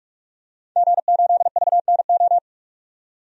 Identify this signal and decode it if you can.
Morse code, heard as a keyed tone, about 46 wpm, 700 Hz. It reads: G8VNO
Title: DXCC-100 Call Signs - 5 Characters - 45wpm